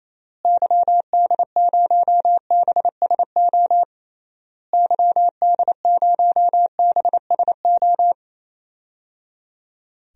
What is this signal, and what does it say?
Morse code, heard as a keyed tone, about 28 wpm, 700 Hz. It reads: YD0BSO YD0BSO
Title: DXCC-100 Call Signs - 6 Characters - Repeated 3x - 28wpm